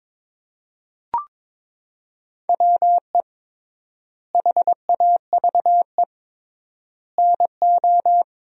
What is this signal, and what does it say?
Morse code, heard as a keyed tone, about 22 wpm, 700 Hz. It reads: WE HAVE NO
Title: 3x Word-Spacing - Sentences from Top 100 Words - 22wpm